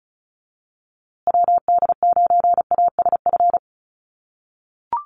Morse code, heard as a keyed tone, about 34 wpm, 700 Hz. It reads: WD9ASF
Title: USA Call Signs Repeated 2x 35wpm